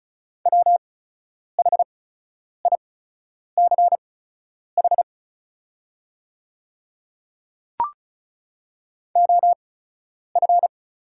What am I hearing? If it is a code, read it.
Morse code, heard as a keyed tone, about 36 wpm, spaced as for 10 wpm, 700 Hz. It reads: WHICH OF
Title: 8x Character Spacing - Top 100 Words - 35wpm